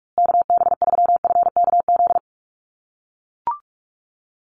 Morse code, heard as a keyed tone, about 44 wpm, 700 Hz. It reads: KB3FKZ
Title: USA Call Signs Repeated 3x 45wpm